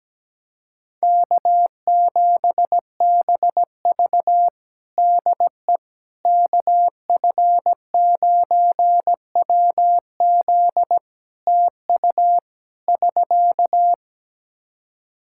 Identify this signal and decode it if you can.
Morse code, heard as a keyed tone, about 17 wpm, 700 Hz. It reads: K7BV DE KF9WZ TU <SK>
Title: K1USN SST QSO Practice 17wpm